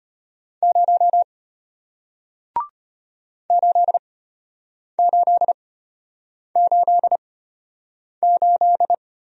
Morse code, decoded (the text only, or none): 0 8888